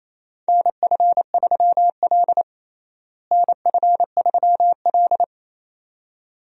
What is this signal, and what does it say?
Morse code, heard as a keyed tone, about 28 wpm, 700 Hz. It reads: NF3L NF3L
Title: USA Call Signs - 4 Characters - Repeated 2x - 28wpm